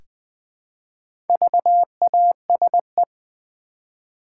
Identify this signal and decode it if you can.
Morse code, heard as a keyed tone, about 20 wpm, 700 Hz. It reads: VASE
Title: Easy Words with Hints - 20wpm